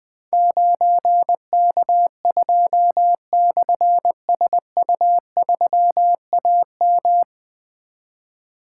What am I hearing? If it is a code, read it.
Morse code, heard as a keyed tone, about 20 wpm, 700 Hz. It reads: 9K2/SU3AM
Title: Difficult Call Signs 20wpm